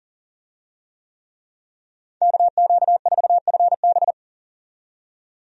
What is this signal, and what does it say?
Morse code, heard as a keyed tone, about 39 wpm, 700 Hz. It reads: KQ4FB
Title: USA Call Signs - 5 Characters - 40wpm